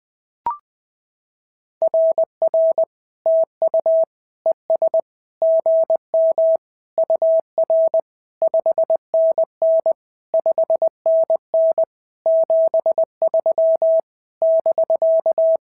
Morse code, heard as a keyed tone, about 20 wpm, 650 Hz. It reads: RR TU ES GM UR 5NN 5NN 73 <BK>
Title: SOTA QSO Practice 20wpm